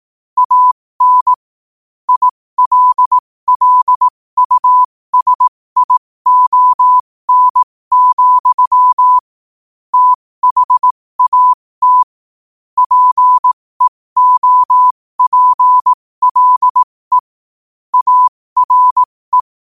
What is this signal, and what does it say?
Morse code, heard as a keyed tone, about 18 wpm, spaced as for 15 wpm, 1000 Hz. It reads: AN ILLUSION, THAT PEOPLE ARE
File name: qotdfiles / 15wpm